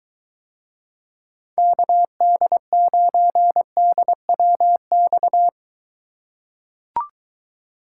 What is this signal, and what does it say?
Morse code, heard as a keyed tone, about 23 wpm, 700 Hz. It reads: KD9DWX